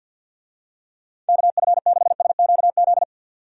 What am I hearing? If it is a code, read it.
Morse code, heard as a keyed tone, about 46 wpm, 700 Hz. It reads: KF6SQZ
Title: USA Call Signs - 6 Characters - 50wpm